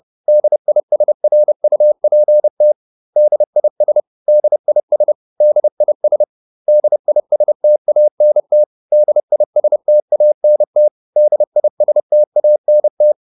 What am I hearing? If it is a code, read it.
Morse code, heard as a keyed tone, about 30 wpm, 600 Hz. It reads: DISRUPT DIS DIS DIS DISTANT DISTANT DISTANT